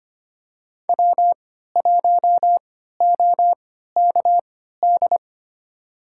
Morse code, heard as a keyed tone, about 25 wpm, spaced as for 14 wpm, 700 Hz. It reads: W1OKD
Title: USA Call Signs - 5 Characters - 3x Character Spacing - 25wpm